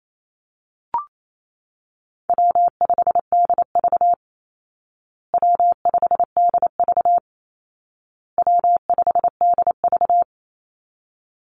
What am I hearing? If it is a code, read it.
Morse code, heard as a keyed tone, about 28 wpm, 700 Hz. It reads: W5DV W5DV W5DV
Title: Easy Call Signs Repeated 3x - 28wpm